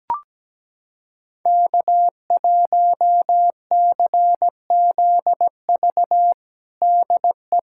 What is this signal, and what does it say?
Morse code, heard as a keyed tone, about 17 wpm, 700 Hz. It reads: K1CZV DE
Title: USA Callsign DE Callsign - 5 Characters - 17wpm